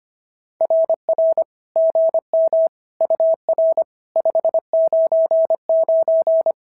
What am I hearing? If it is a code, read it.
Morse code, heard as a keyed tone, about 25 wpm, 590 Hz. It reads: RR GM UR 599
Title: SOTA QSO Practice 25wpm